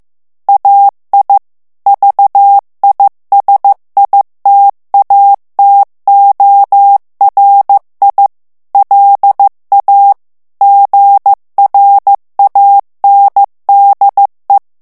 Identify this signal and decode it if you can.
Morse code, heard as a keyed tone, about 15 wpm, 800 Hz. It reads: AI VISITATORI LA GRANDE
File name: ITALIEN texte clair 1 vitesse 14 mots minute